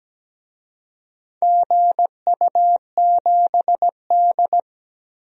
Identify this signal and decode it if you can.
Morse code, heard as a keyed tone, about 17 wpm, 700 Hz. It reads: GU7D
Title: DXCC-100 Call Signs - 4 Characters - 17wpm